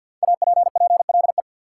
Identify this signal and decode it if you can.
Morse code, heard as a keyed tone, about 49 wpm, 700 Hz. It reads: APPLE
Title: Five Letter Words - 50wpm